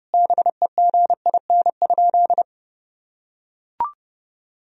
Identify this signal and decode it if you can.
Morse code, heard as a keyed tone, about 30 wpm, 700 Hz. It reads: BEGIN?
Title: Sentences from Top 1000 Words - Review Part 4 - 30wpm